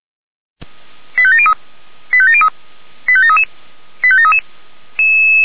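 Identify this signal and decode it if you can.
Five-tone selective call (ZVEI), heard as quick runs of five tones, tones of about 70 ms. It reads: 85703 85703 85733 85733
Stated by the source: ZVEI five-tone selective call received on SDR, fire brigade alerting, Feuerwehr Orlamünde.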